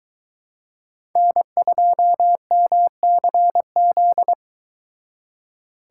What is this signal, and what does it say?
Morse code, heard as a keyed tone, about 22 wpm, 700 Hz. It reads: N2MCZ